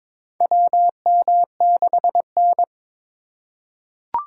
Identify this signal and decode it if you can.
Morse code, heard as a keyed tone, about 22 wpm, 700 Hz. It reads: WM6N
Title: USA Call Signs - 4 Characters - 3x Character Spacing - 22wpm